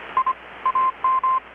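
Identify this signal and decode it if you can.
Morse code, heard as a keyed tone, about 25 wpm, 1040 Hz, with radio noise. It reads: I AM
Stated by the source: Morse from KPH's Night of Nights XX maritime radiotelegraphy event, received in Chiapas, Mexico about 3,800 km away on a Kenwood R-600 with a grounded 10 m long-wire antenna.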